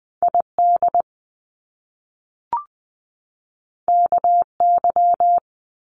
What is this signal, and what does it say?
Morse code, heard as a keyed tone, about 20 wpm, 700 Hz. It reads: ID KY
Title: US States 20wpm